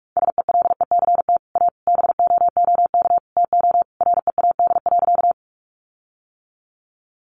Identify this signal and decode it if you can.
Morse code, heard as a keyed tone, about 44 wpm, 700 Hz. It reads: SELECT A BOOK TO READ.
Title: Sentences from Top 1300 Words 45wpm